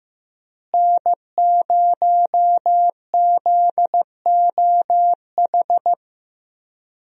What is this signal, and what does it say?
Morse code, heard as a keyed tone, about 15 wpm, 700 Hz. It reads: N0ZOH
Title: USA Call Signs - 5 Characters - Repeated 2x - 15wpm